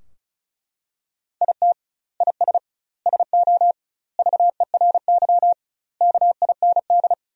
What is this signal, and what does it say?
Morse code, heard as a keyed tone, about 36 wpm, 700 Hz. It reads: IT IS SO VERY KIND
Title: 2x Word-Spacing - Sentences from Top 300 Words - 35wpm